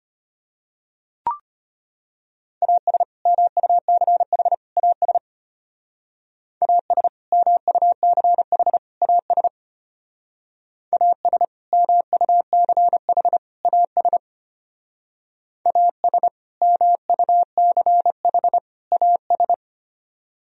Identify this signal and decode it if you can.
Morse code, heard as a keyed tone, about 32 wpm, 700 Hz. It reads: AS MUCH AS AS MUCH AS AS MUCH AS AS MUCH AS